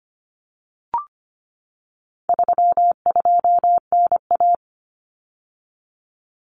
Morse code, heard as a keyed tone, about 25 wpm, 700 Hz. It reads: 32NA
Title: Common Cut Numbers - 4-Digit 25wpm